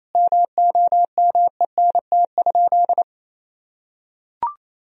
Morse code, heard as a keyed tone, about 28 wpm, 700 Hz. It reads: MOMENT?
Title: Sentences from Top 1000 Words - Review Part 1 - 28wpm